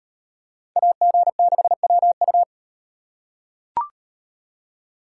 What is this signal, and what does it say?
Morse code, heard as a keyed tone, about 38 wpm, 700 Hz. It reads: AG6WU